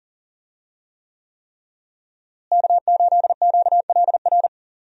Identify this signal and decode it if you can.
Morse code, heard as a keyed tone, about 40 wpm, 700 Hz. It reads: K8QLR
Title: USA Call Signs - 5 Characters - Repeated 3x - 40wpm